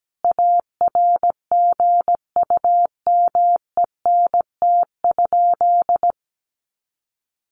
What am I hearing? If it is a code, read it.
Morse code, heard as a keyed tone, about 17 wpm, 700 Hz. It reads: ARGUMENT?
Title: Sentences from Top 1400 Words 17wpm